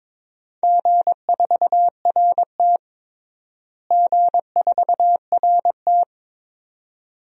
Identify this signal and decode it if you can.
Morse code, heard as a keyed tone, about 22 wpm, 700 Hz. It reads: G4RT G4RT
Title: DXCC-100 Call Signs - 4 Characters - Repeated 2x - 22wpm